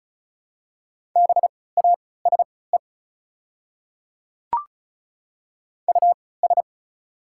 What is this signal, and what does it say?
Morse code, heard as a keyed tone, about 36 wpm, spaced as for 20 wpm, 700 Hz. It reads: BASE US
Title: 3x Character Spacing - Top 500 Words - 35wpm